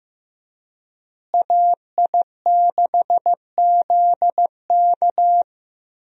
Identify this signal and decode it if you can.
Morse code, heard as a keyed tone, about 15 wpm, 700 Hz. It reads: AI6ZK
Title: USA Call Signs - 5 Characters - Repeated 3x - 15wpm